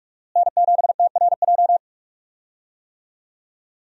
Morse code, heard as a keyed tone, about 46 wpm, 700 Hz. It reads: N7TRJ
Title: USA Call Signs - 5 Characters - Repeated 3x - 45wpm